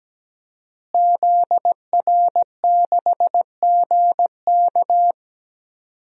Morse code, heard as a keyed tone, about 17 wpm, 700 Hz. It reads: ZR6GK
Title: DXCC-100 Call Signs - 5 Characters - Repeated 3x - 17wpm